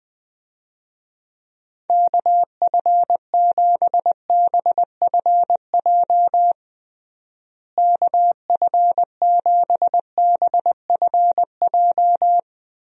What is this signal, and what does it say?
Morse code, heard as a keyed tone, about 20 wpm, 700 Hz. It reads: KF7BFJ KF7BFJ
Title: USA Call Signs - 6 Characters - Repeated 2x - 20wpm